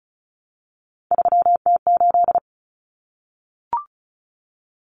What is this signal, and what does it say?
Morse code, heard as a keyed tone, about 34 wpm, 700 Hz. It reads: 3T8
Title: Common Cut Numbers - 3-Digit 35wpm